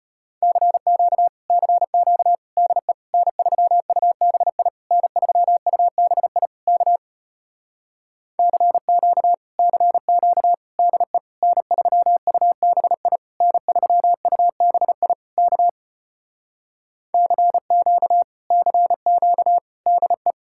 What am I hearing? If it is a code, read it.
Morse code, heard as a keyed tone, about 33 wpm, 700 Hz. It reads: CQ CQ DE N3UBI N3UBI K CQ CQ DE N3UBI N3UBI K CQ CQ DE